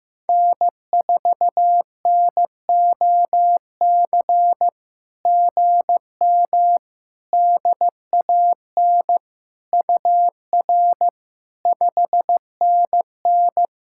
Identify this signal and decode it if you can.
Morse code, heard as a keyed tone, about 14 wpm, 700 Hz. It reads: N4NOC GM DAN UR 5NN
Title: SOTA QSO Practice 15wpm